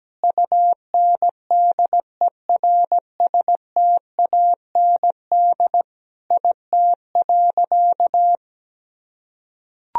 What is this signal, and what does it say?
Morse code, heard as a keyed tone, about 17 wpm, 700 Hz. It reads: UNDERSTAND IT.
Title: Sentences from Top 1000 Words - Review Part 4 - 17wpm